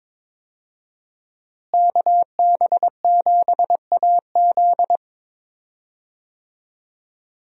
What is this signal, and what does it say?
Morse code, heard as a keyed tone, about 22 wpm, 700 Hz. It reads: KB7AZ
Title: USA Call Signs - 5 Characters - Repeated 3x - 22wpm